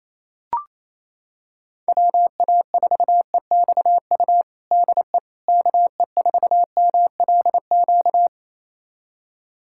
Morse code, heard as a keyed tone, about 27 wpm, 700 Hz. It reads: WA4EXU DE KE4MLQ
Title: USA Callsign DE Callsign - 6 Characters - 28wpm